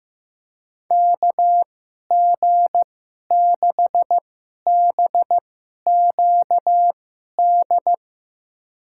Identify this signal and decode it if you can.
Morse code, heard as a keyed tone, about 15 wpm, spaced as for 11 wpm, 700 Hz. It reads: KG6BQD
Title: USA Call Signs - 6 Characters - 2x Character Spacing - 15wpm